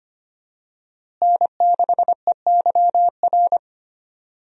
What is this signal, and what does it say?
Morse code, heard as a keyed tone, about 25 wpm, 700 Hz. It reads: N6EYR
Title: USA Call Signs - 5 Characters - Repeated 2x - 25wpm